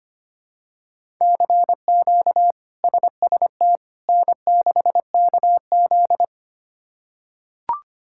Morse code, heard as a keyed tone, about 25 wpm, 700 Hz. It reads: CQ SST N6KZ
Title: K1USN SST QSO Practice - 25wpm